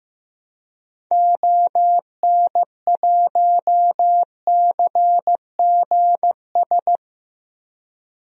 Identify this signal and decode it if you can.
Morse code, heard as a keyed tone, about 15 wpm, 700 Hz. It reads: ON1CGS
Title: DXCC-100 Call Signs - 6 Characters - 15wpm